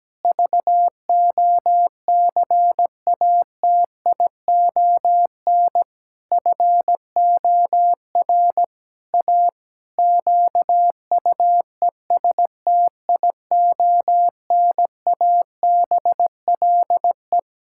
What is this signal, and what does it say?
Morse code, heard as a keyed tone, about 17 wpm, 700 Hz. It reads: VOCATION FOR A QUESTIONABLE
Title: Pangram Challenge v2 - 17wpm